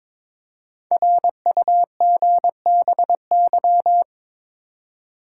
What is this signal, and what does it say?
Morse code, heard as a keyed tone, about 22 wpm, 700 Hz. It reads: RUGBY
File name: 5 Letter Words - First 3 Letters Given - 22wpm